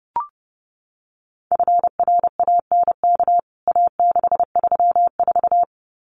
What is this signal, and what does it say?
Morse code, heard as a keyed tone, about 30 wpm, 700 Hz. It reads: FRANK A634
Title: CWT Exchange 30wpm